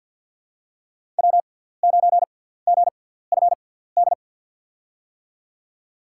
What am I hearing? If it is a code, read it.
Morse code, heard as a keyed tone, about 47 wpm, spaced as for 17 wpm, 700 Hz. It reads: W9GFD
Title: USA Call Signs - 5 Characters - 6x Character Spacing - 50wpm